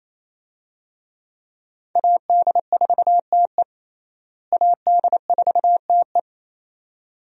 Morse code, heard as a keyed tone, about 28 wpm, 700 Hz. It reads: AD4TE AD4TE
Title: USA Call Signs - 5 Characters - Repeated 2x - 28wpm